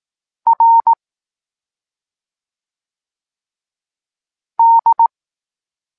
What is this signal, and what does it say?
Morse code, heard as a keyed tone, about 18 wpm, 920 Hz. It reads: R D
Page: Tekst 23.3 – Ciąg dalszy doskonalenia odbioru poznanych liter